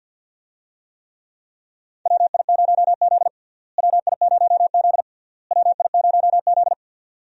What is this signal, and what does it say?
Morse code, heard as a keyed tone, about 51 wpm, 700 Hz. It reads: WI0Z WI0Z WI0Z
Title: USA Call Signs - 4 Characters - Repeated 3x - 50wpm